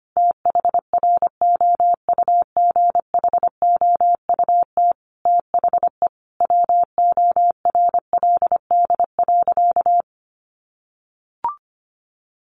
Morse code, heard as a keyed tone, about 25 wpm, 700 Hz. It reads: THROUGHOUT THE WORLD.
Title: Sentences from Top 1500 Words 25wpm